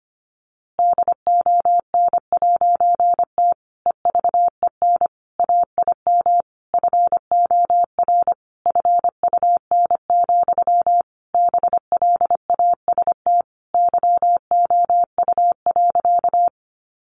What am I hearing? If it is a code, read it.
Morse code, heard as a keyed tone, about 25 wpm, 700 Hz. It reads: DON'T EVEN AIM FOR FUN, BLAST YOU.
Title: A Sound Of Thunder 25wpm